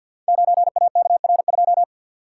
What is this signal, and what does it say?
Morse code, heard as a keyed tone, about 50 wpm, 700 Hz. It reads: 9AKR2
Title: Code Groups - 5 Letter:Numbers - 50wpm